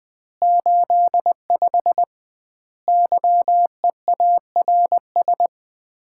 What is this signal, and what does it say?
Morse code, heard as a keyed tone, about 20 wpm, 700 Hz. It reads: 85 YEARS
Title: Common QSO Phrases - 2x Word-Spacing - 20wpm